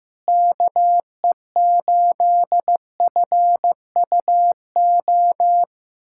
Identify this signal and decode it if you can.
Morse code, heard as a keyed tone, about 15 wpm, 700 Hz. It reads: KE8FUO
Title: USA Callsign DE Callsign - 6 Characters - 15wpm